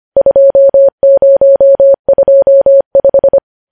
Morse code, heard as a keyed tone, about 25 wpm, 550 Hz. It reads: 2025
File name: podcast / News headlines-1x-25x25